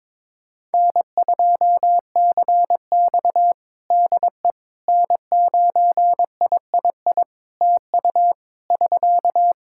Morse code, heard as a keyed tone, about 22 wpm, 700 Hz. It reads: N2CX DE N9III TU <SK>
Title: K1USN SST QSO Practice 22wpm